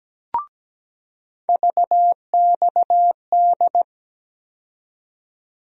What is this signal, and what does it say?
Morse code, heard as a keyed tone, about 17 wpm, 700 Hz. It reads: VXD
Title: USA Call Sign Postfixes - 3 Letters 17wpm